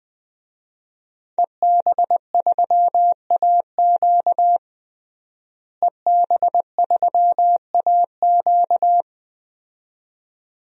Morse code, heard as a keyed tone, about 20 wpm, 700 Hz. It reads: EB3AQ EB3AQ
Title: DXCC-100 Call Signs - 5 Characters - Repeated 2x - 20wpm